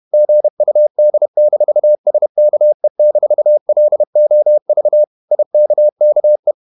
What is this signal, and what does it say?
Morse code, heard as a keyed tone, about 31 wpm, 600 Hz. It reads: GUD-SKE-LOV IKKE
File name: Ferie QRQ 31wpm